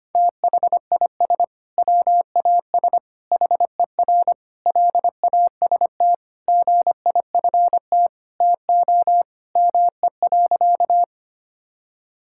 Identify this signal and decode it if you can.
Morse code, heard as a keyed tone, about 25 wpm, 700 Hz. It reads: THIS WAS HER LAST GIFT TO ME.
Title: Sentences from Top 1000 Words - Review Part 6 - 25wpm